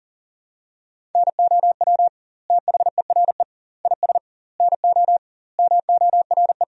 Morse code, heard as a keyed tone, about 40 wpm, 700 Hz. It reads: NOW THERE IS NO MORE
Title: 2x Word-Spacing - Sentences from Top 100 Words - 40wpm